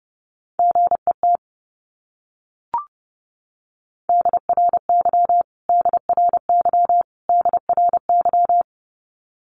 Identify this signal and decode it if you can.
Morse code, heard as a keyed tone, about 30 wpm, 700 Hz. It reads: GET DRY DRY DRY
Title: Top 500 Words Repeated 30wpm